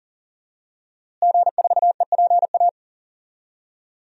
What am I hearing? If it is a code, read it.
Morse code, heard as a keyed tone, about 40 wpm, 700 Hz. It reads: G4EPA
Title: DXCC-100 Call Signs - 5 Characters - Repeated 2x - 40wpm